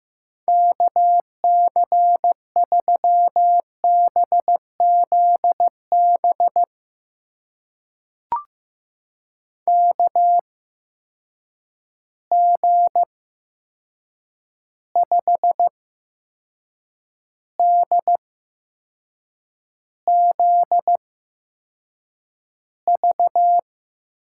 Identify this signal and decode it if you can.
Morse code, heard as a keyed tone, about 15 wpm, spaced as for 6 wpm, 700 Hz. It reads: KC3BZB KG5DZV
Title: USA Call Signs - 6 Characters - 8x Character Spacing - 15wpm